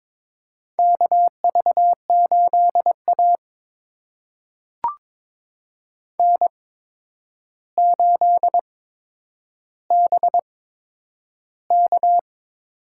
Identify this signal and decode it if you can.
Morse code, heard as a keyed tone, about 22 wpm, spaced as for 9 wpm, 700 Hz. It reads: KV8A N8BK
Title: USA Call Signs - 4 Characters - 8x Character Spacing - 22wpm